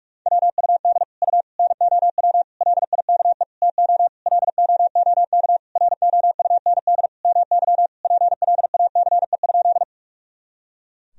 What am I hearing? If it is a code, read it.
Morse code, heard as a keyed tone, about 46 wpm, 700 Hz. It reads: WUD U NOW LIKE TO LOOK ROUND MY PLACE?
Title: Sentences from Top 200 Words and Abbreviations 45wpm